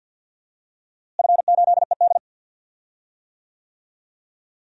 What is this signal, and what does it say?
Morse code, heard as a keyed tone, about 52 wpm, 700 Hz. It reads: F8ED
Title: DXCC-100 Call Signs - 4 Characters - Repeated 3x - 50wpm